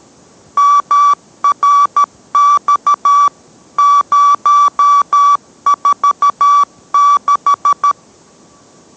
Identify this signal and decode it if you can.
Morse code, heard as a keyed tone, about 13 wpm, 1150 Hz. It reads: MRX 046